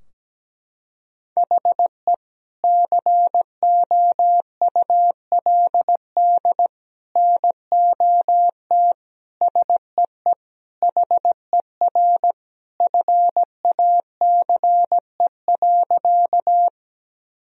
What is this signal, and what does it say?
Morse code, heard as a keyed tone, about 17 wpm, 700 Hz. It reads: HE COULD NOT SEE HER FACE.
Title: Sentences from Top 1000 Words - Review Part 2 - 17wpm